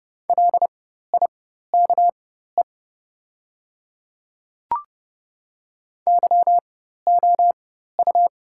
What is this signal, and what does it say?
Morse code, heard as a keyed tone, about 30 wpm, spaced as for 14 wpm, 700 Hz. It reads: LIKE YOU
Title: 4x Character Spacing - Top 100 Words - 30wpm